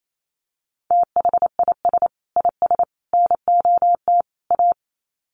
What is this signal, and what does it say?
Morse code, heard as a keyed tone, about 28 wpm, 700 Hz. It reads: THIS IS NOT A
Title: Sets of 4 Words 28wpm